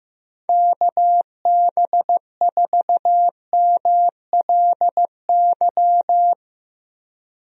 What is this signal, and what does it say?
Morse code, heard as a keyed tone, about 15 wpm, 700 Hz. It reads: KB4MLY
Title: USA Call Signs - 6 Characters - Repeated 2x - 15wpm